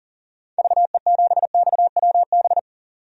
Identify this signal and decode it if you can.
Morse code, heard as a keyed tone, about 40 wpm, 700 Hz. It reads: VE7XWB
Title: DXCC-100 Call Signs - 6 Characters - 40wpm